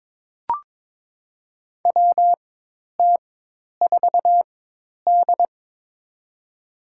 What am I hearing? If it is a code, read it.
Morse code, heard as a keyed tone, about 22 wpm, spaced as for 10 wpm, 700 Hz. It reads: WT4D
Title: USA Call Signs - 4 Characters - 4x Character Spacing - 22wpm